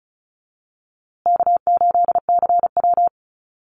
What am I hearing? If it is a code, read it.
Morse code, heard as a keyed tone, about 34 wpm, 700 Hz. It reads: K8CW
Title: USA Call Signs Repeated 3x 35wpm